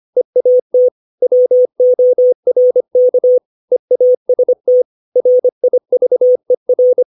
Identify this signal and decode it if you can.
Morse code, heard as a keyed tone, about 25 wpm, 500 Hz. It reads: EAT WORK EAST RIVER
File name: 632 words / 632 Words 25WPM